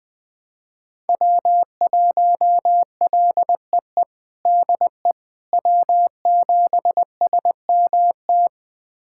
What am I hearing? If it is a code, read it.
Morse code, heard as a keyed tone, about 21 wpm, 700 Hz. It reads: W1LEE DE W7SMT
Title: USA Callsign DE Callsign - 5 Characters - 20wpm